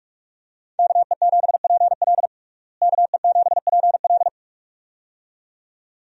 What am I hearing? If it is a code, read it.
Morse code, heard as a keyed tone, about 46 wpm, 700 Hz. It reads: KE7PL KE7PL
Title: USA Call Signs - 5 Characters - Repeated 2x - 45wpm